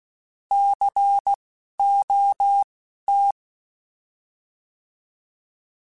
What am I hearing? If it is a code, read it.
Morse code, heard as a keyed tone, about 16 wpm, spaced as for 12 wpm, 800 Hz. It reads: COT
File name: Alphabetical / Alphabetical mp3s / Cs @10